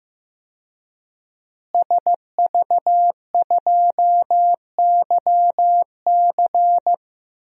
Morse code, heard as a keyed tone, about 15 wpm, 700 Hz. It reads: SV2YC
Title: DXCC-100 Call Signs - 5 Characters - Repeated 3x - 15wpm